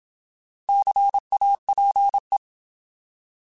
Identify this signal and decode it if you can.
Morse code, heard as a keyed tone, about 27 wpm, 800 Hz. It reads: CAPE
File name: Alphabetical